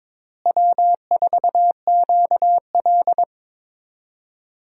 Morse code, heard as a keyed tone, about 22 wpm, 700 Hz. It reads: W4QL
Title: USA Call Signs - 4 Characters - Repeated 3x - 22wpm